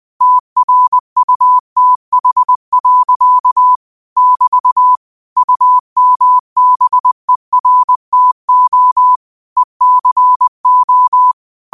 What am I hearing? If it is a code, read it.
Morse code, heard as a keyed tone, about 20 wpm, 1000 Hz. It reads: TRUTH. <BT> UMBERTO ECO
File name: qotdfiles / 20wpm